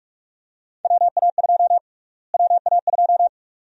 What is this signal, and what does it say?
Morse code, heard as a keyed tone, about 44 wpm, 700 Hz. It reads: WA2 WA2
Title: USA Call Sign Prefixes - Repeated 2x - 45wpm